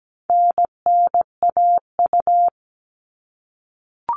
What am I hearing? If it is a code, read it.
Morse code, heard as a keyed tone, about 17 wpm, 700 Hz. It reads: NNAU
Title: All Cut Numbers - 4-Digit 17wpm